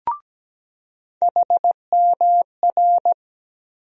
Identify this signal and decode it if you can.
Morse code, heard as a keyed tone, about 17 wpm, 700 Hz. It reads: HMR
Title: USA Call Sign Postfixes - 3 Letters 17wpm